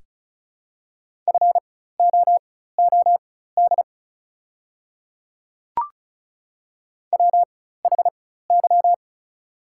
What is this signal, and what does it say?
Morse code, heard as a keyed tone, about 36 wpm, spaced as for 16 wpm, 690 Hz. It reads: FOOD WHY
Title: 4x Character Spacing - Top 300 Words - 35wpm